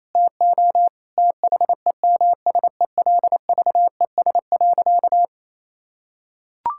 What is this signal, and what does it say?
Morse code, heard as a keyed tone, about 28 wpm, 700 Hz. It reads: TO THEMSELVES.
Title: Sentences from Top 1000 Words - Review Part 6 - 28wpm